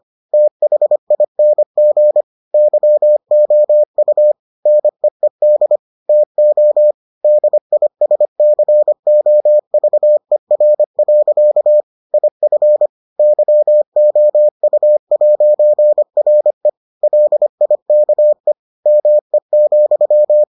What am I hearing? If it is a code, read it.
Morse code, heard as a keyed tone, about 25 wpm, 600 Hz. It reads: THING YOU NEED TO DISCOVER. IF YOU'RE LIKE ME,